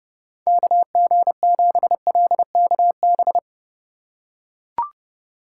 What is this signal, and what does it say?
Morse code, heard as a keyed tone, about 29 wpm, 700 Hz. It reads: KG7LKB